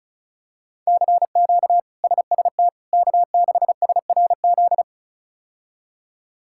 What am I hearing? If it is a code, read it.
Morse code, heard as a keyed tone, about 34 wpm, 700 Hz. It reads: CQ SST K6SRZ
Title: K1USN SST QSO Practice - 35wpm